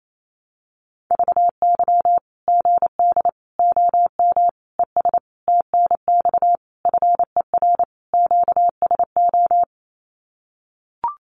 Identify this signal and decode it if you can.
Morse code, heard as a keyed tone, about 28 wpm, 700 Hz. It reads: VY GD OM ES TNX FER QSO
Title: Common QSO Phrases 28wpm